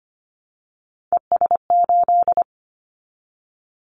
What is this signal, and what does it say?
Morse code, heard as a keyed tone, about 25 wpm, 700 Hz. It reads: ES8
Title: Callsign Prefixes for Top 50 DXCC - 25wpm